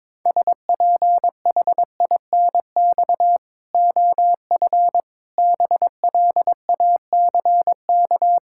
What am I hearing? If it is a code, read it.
Morse code, heard as a keyed tone, about 22 wpm, 700 Hz. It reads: SPHINX OF BLACK
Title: Pangram Challenge v2 - 22wpm